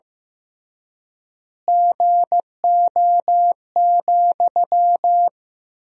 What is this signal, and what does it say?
Morse code, heard as a keyed tone, about 16 wpm, 700 Hz. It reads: GO,
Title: 3x Word-Spacing - Sentences from Top 400 Words - 15wpm